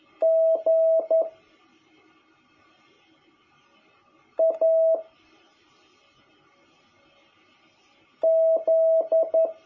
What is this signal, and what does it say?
Morse code, heard as a keyed tone, about 11 wpm, 660 Hz. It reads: G A Z
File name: jh8cbh / 2021cw